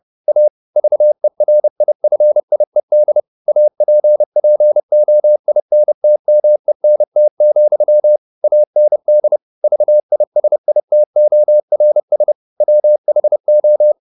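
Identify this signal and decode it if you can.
Morse code, heard as a keyed tone, about 30 wpm, 600 Hz. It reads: A VERIFIED APPOINTMENT, AND VISITORS WHO
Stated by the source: Morse code at 29 wpm, sending Ken Follett's novel Never in English, from CW med Gnister.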